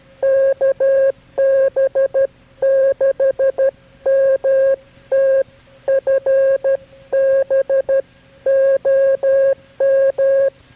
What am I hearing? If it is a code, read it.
Morse code, heard as a keyed tone, about 13 wpm, 540 Hz. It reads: KB6MT FB OM
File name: KB6MT-tape3-side1